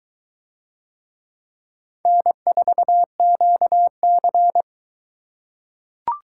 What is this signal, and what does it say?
Morse code, heard as a keyed tone, about 22 wpm, 700 Hz. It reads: N4QC